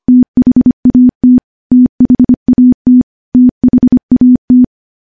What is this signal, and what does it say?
Morse code, heard as a keyed tone, about 25 wpm, 250 Hz. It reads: THAT THAT THAT
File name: morse-training-25WPM